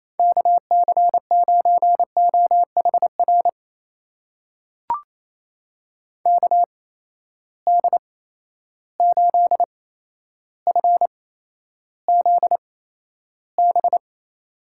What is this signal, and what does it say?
Morse code, heard as a keyed tone, about 28 wpm, spaced as for 12 wpm, 700 Hz. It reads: KC9OHR KD8FZB
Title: USA Call Signs - 6 Characters - 8x Character Spacing - 28wpm